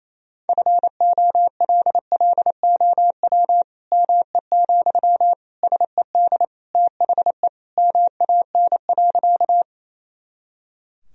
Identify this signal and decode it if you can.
Morse code, heard as a keyed tone, about 28 wpm, 700 Hz. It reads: FOLLOW ME, SED THE MAN.
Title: Sentences from Top 200 Words and Abbreviations 28wpm